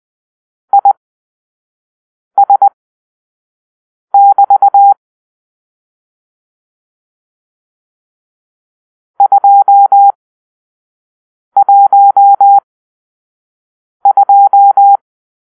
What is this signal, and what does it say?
Morse code, heard as a keyed tone, about 20 wpm, spaced as for 5 wpm, 800 Hz. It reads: IS<BT> 212
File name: arrl / Grp-6 Drill-4